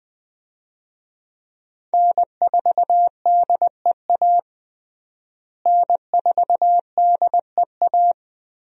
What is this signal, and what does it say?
Morse code, heard as a keyed tone, about 20 wpm, 700 Hz. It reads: N4DEA N4DEA
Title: USA Call Signs - 5 Characters - Repeated 3x - 20wpm